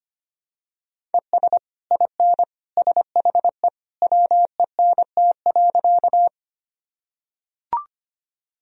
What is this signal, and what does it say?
Morse code, heard as a keyed tone, about 25 wpm, 700 Hz. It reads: ES IN SHE WENT.
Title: Sentences from Top 200 Words and Abbreviations 25wpm